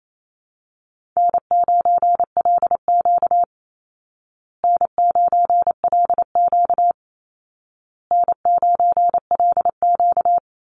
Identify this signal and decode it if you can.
Morse code, heard as a keyed tone, about 28 wpm, 700 Hz. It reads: N9LQ N9LQ N9LQ
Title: Call Signs Repeated 2x - 28wpm